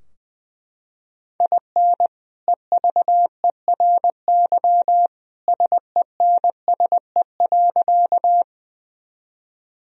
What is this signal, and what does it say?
Morse code, heard as a keyed tone, about 21 wpm, 700 Hz. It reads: IN EVERY SENSE.
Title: Sentences from Top 1000 Words - Review Part 4 - 20wpm